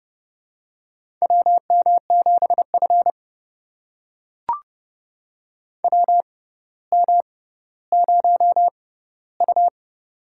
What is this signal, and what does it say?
Morse code, heard as a keyed tone, about 30 wpm, spaced as for 15 wpm, 700 Hz. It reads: WM7F WM0U
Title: USA Call Signs - 4 Characters - 6x Character Spacing - 30wpm